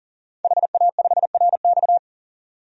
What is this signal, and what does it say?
Morse code, heard as a keyed tone, about 40 wpm, 700 Hz. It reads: HA5RX
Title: DXCC-100 Call Signs - 5 Characters - 40wpm